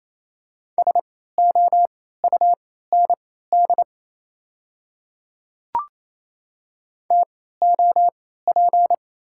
Morse code, heard as a keyed tone, about 28 wpm, spaced as for 16 wpm, 690 Hz. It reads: SOUND TOP
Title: 3x Character Spacing - Top 500 Words - 28wpm